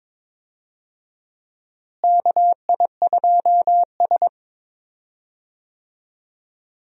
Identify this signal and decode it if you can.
Morse code, heard as a keyed tone, about 21 wpm, 700 Hz. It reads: KI2S
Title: USA Call Signs - 4 Characters - Repeated 2x - 22wpm